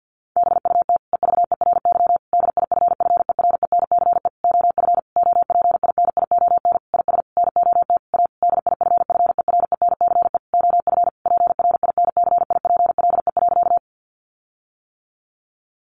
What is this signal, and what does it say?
Morse code, heard as a keyed tone, about 50 wpm, 700 Hz. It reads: BUT EVERY DIFFERENCE OF OPINION IS NOT A DIFFERENCE OF PRINCIPLE.
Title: Sentences from Top 1400 Words 50wpm